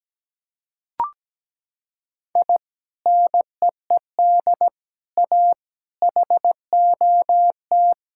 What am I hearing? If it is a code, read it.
Morse code, heard as a keyed tone, about 17 wpm, 700 Hz. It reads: I NEED A HOT
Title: Sentences from Top 1600 Words 17wpm